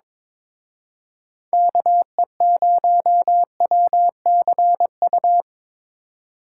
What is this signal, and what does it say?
Morse code, heard as a keyed tone, about 25 wpm, 700 Hz. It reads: KE0WCU
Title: Speed Racing - USA 2x3 Call Signs 22wpm